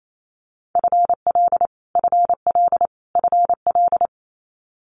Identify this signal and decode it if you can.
Morse code, heard as a keyed tone, about 28 wpm, 700 Hz. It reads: FL FL FL
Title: Multi-letter Phonemes Repeated 3x 28wpm